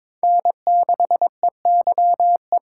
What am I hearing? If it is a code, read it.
Morse code, heard as a keyed tone, about 22 wpm, 700 Hz. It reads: N6EYE
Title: USA Call Signs - 5 Characters - 2x Character Spacing - 22wpm